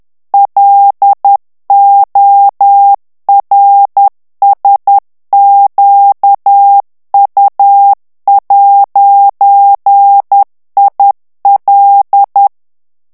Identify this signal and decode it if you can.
Morse code, heard as a keyed tone, about 10 wpm, 800 Hz. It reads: LORSQU'IL
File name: morse mp3 / FRANCAIS TEXTE CLAIR 2 VITESSE 10 MOTS MINUTES